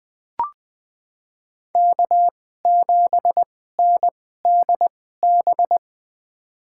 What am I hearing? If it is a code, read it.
Morse code, heard as a keyed tone, about 21 wpm, spaced as for 15 wpm, 700 Hz. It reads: K7NDB
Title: USA Call Signs - 5 Characters - 2x Character Spacing - 20wpm